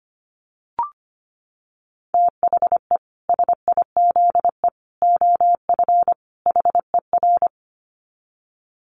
Morse code, heard as a keyed tone, about 25 wpm, 700 Hz. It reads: THE SIZE OF HER
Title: Dit Challenge 25wpm